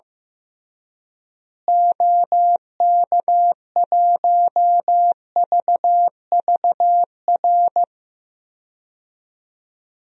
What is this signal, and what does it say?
Morse code, heard as a keyed tone, about 16 wpm, 700 Hz. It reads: OK1VVR
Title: DXCC-100 Call Signs - 6 Characters - Repeated 3x - 15wpm